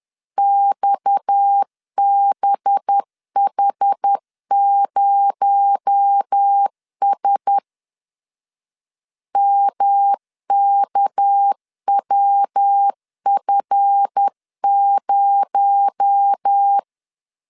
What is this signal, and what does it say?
Morse code, heard as a keyed tone, about 10 wpm, 800 Hz. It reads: XBH0S MKWF0
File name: LEZIONI / 7 GF9Z / 7 GF9 10wpm 2747